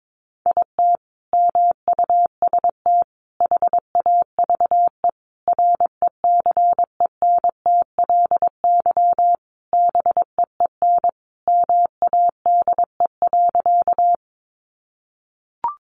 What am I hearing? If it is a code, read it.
Morse code, heard as a keyed tone, about 22 wpm, 700 Hz. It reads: IT MUST HAVE RECENTLY BEEN MADE.
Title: Sentences from Top 1200 Words 22wpm